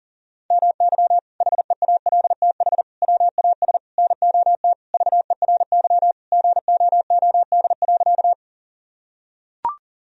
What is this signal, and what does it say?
Morse code, heard as a keyed tone, about 37 wpm, 700 Hz. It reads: MY HEALTH WAS NOT VERY GOOD.
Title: Sentences from Top 1000 Words - Review Part 3 - 40wpm